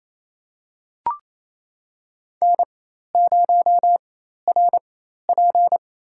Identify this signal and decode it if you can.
Morse code, heard as a keyed tone, about 27 wpm, spaced as for 13 wpm, 700 Hz. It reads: N0RP
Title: USA Call Signs - 4 Characters - 4x Character Spacing - 28wpm